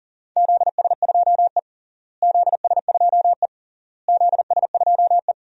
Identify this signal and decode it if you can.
Morse code, heard as a keyed tone, about 40 wpm, 700 Hz. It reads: ZS2E ZS2E ZS2E
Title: DXCC-100 Call Signs - 4 Characters - Repeated 3x - 40wpm